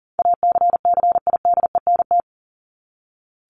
Morse code, heard as a keyed tone, about 40 wpm, 700 Hz. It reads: ACCIDENT
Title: Top 1751-2000 Words 40wpm